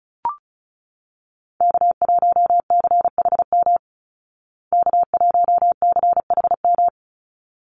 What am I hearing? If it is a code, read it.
Morse code, heard as a keyed tone, about 34 wpm, 700 Hz. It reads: K1CHM K1CHM
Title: Call Signs Repeated 3x - 35wpm